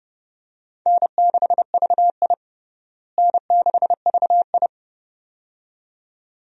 Morse code, heard as a keyed tone, about 30 wpm, 700 Hz. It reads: N6VI N6VI
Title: USA Call Signs - 4 Characters - Repeated 2x - 30wpm